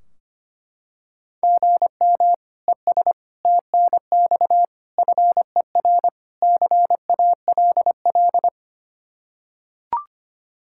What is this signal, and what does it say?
Morse code, heard as a keyed tone, about 25 wpm, 700 Hz. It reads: GM ES TNX FER CALL
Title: Practice QSO - Exchange 1 - Beginning - 25wpm